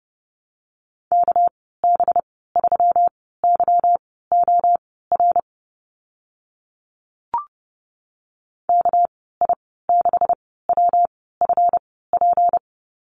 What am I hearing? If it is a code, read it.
Morse code, heard as a keyed tone, about 30 wpm, spaced as for 17 wpm, 700 Hz. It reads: KB3YOR KI6WFP
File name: Lesson 37g - Letter x - Call Signs 30wpm